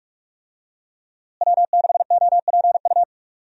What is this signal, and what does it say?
Morse code, heard as a keyed tone, about 46 wpm, 700 Hz. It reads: W6OPU
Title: USA Call Signs - 5 Characters - Repeated 2x - 45wpm